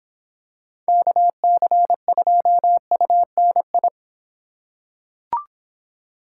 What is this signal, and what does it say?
Morse code, heard as a keyed tone, about 25 wpm, 700 Hz. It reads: KC2UNI